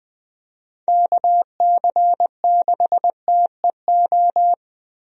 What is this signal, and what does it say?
Morse code, heard as a keyed tone, about 20 wpm, 700 Hz. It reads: KC6TEO
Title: USA Call Signs - 6 Characters - 20wpm